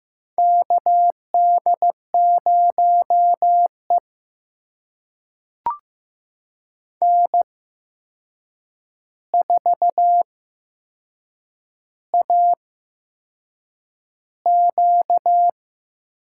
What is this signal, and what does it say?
Morse code, heard as a keyed tone, about 15 wpm, spaced as for 6 wpm, 700 Hz. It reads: KD0E N4AQ
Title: USA Call Signs - 4 Characters - 8x Character Spacing - 15wpm